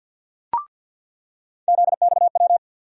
Morse code, heard as a keyed tone, about 46 wpm, 700 Hz. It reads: ZXW
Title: USA Call Sign Postfixes - 3 Letters 50wpm